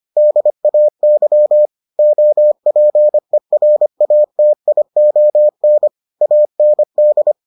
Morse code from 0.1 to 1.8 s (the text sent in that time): DAY